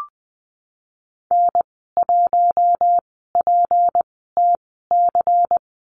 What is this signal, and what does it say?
Morse code, heard as a keyed tone, about 20 wpm, spaced as for 15 wpm, 700 Hz. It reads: N1PTC
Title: 173 Lesson 28g - Number 9 - Call Signs 20wpm